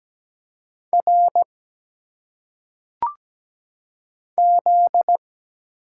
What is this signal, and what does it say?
Morse code, heard as a keyed tone, about 17 wpm, 700 Hz. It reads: R Z
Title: Single Letters - Phonetic Pronunciation 17wpm